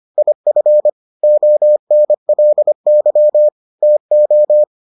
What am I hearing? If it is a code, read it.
Morse code, heard as a keyed tone, about 25 wpm, 600 Hz. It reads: IF ONLY TO